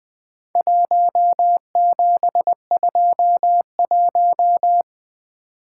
Morse code, heard as a keyed tone, about 22 wpm, 700 Hz. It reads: 1721
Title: Speed-Racing - Four Letter-Number Combinations - 20wpm